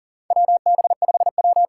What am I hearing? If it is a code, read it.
Morse code, heard as a keyed tone, about 42 wpm, 700 Hz. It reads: WB5W
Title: USA Call Signs - 8x Character Spacing - 40wpm